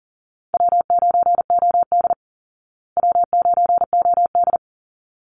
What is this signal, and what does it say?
Morse code, heard as a keyed tone, about 40 wpm, 700 Hz. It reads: W9OD W9OD
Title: Easy Call Signs Repeated 2x - 40wpm